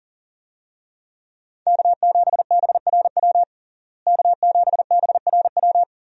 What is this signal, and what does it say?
Morse code, heard as a keyed tone, about 40 wpm, 700 Hz. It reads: K7BRW K7BRW
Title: USA Call Signs - 5 Characters - Repeated 3x - 40wpm